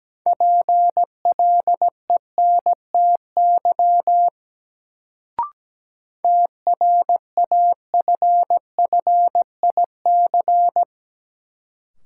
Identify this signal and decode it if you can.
Morse code, heard as a keyed tone, about 17 wpm, 700 Hz. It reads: PLENTY TRAFFIC
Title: Sentences from Top 1300 Words - New Word Drill - 17wpm